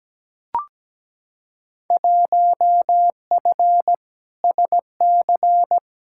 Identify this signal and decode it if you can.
Morse code, heard as a keyed tone, about 17 wpm, 700 Hz. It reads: 1F SC
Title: Field Day Exchanges Encore - 17wpm